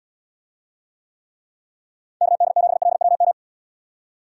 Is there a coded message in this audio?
Yes: Morse code from 2.2 to 3.3 s, heard as a keyed tone, about 79 wpm, 700 Hz. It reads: KD7LWM